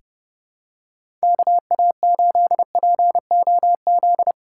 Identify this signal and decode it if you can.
Morse code, heard as a keyed tone, about 30 wpm, 700 Hz. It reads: KA8POZ
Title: K1USN SST QSO Practice - 30wpm